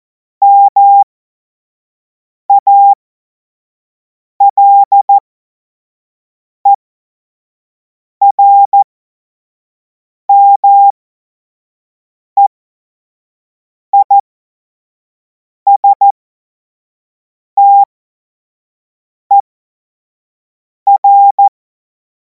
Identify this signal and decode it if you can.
Morse code, heard as a keyed tone, about 14 wpm, spaced as for 5 wpm, 800 Hz. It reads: MALERMEISTER
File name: Morsekurs / Witze / Witz 10. Februar 2026 25 F70